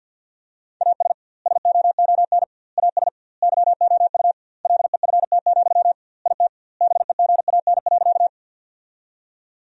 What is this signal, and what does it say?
Morse code, heard as a keyed tone, about 50 wpm, 700 Hz. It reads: AS SOON AS YOU LEFT, IT BEGAN.
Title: 2x Word-Spacing - Sentences from Top 400 Words - 50wpm